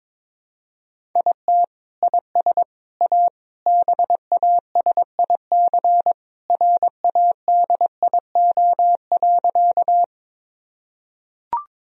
Morse code, heard as a keyed tone, about 22 wpm, 700 Hz. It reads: IT IS A BASIC RADIO.
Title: Sentences from Top 1000 Words - Review Part 3 - 22wpm